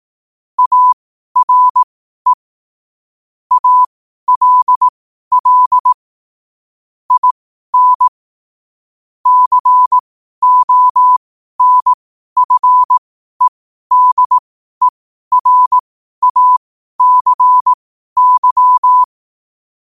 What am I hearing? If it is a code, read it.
Morse code, heard as a keyed tone, about 18 wpm, spaced as for 12 wpm, 1000 Hz. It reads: ARE ALL IN CONFEDERACY
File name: qotdfiles / 12wpm